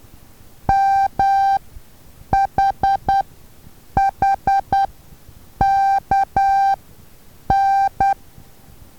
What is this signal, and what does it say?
Morse code, heard as a keyed tone, about 10 wpm, spaced as for 7 wpm, 780 Hz. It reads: MHHKN
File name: Taviro HA6VA B